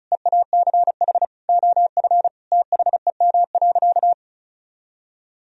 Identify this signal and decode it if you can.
Morse code, heard as a keyed tone, about 34 wpm, 700 Hz. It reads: EACH OF THEM.
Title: Sentences from Top 1000 Words - Review Part 1 - 35wpm